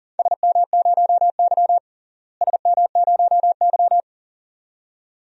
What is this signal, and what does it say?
Morse code, heard as a keyed tone, about 37 wpm, 700 Hz. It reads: SM0Y SM0Y
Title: DXCC-100 Call Signs - 4 Characters - Repeated 2x - 40wpm